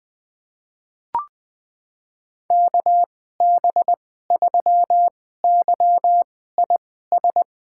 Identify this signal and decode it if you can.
Morse code, heard as a keyed tone, about 21 wpm, spaced as for 15 wpm, 700 Hz. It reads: KB3YIS
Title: USA Call Signs - 6 Characters - 2x Character Spacing - 20wpm